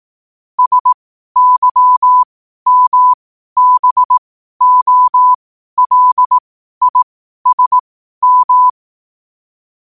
Morse code, heard as a keyed tone, about 18 wpm, spaced as for 12 wpm, 1000 Hz. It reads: SYMBOLISM
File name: qotdfiles / 12wpm